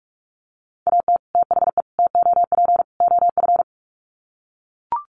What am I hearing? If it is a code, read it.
Morse code, heard as a keyed tone, about 44 wpm, 700 Hz. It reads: AT THE TOP OF
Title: Sets of 4 Words 45wpm